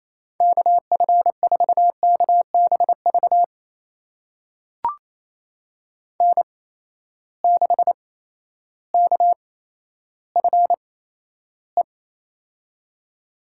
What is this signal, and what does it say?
Morse code, heard as a keyed tone, about 28 wpm, spaced as for 13 wpm, 700 Hz. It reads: KF4KBV N6KFE
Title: USA Call Signs - 8x Character Spacing - 28wpm